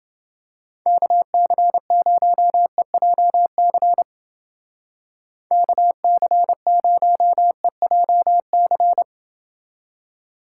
Morse code, heard as a keyed tone, about 28 wpm, 700 Hz. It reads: KC0EJC KC0EJC